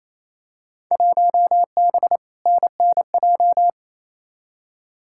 Morse code, heard as a keyed tone, about 27 wpm, 700 Hz. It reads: 1B NNJ
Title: Field Day Exchanges Encore - 28wpm